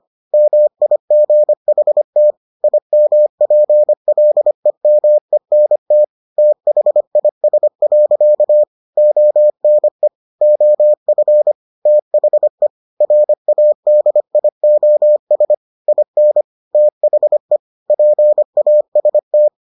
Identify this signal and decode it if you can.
Morse code, heard as a keyed tone, about 25 wpm, 600 Hz. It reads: MIGHT IMPLEMENT THIS. ONE OF THE RADIOS IN THE PAST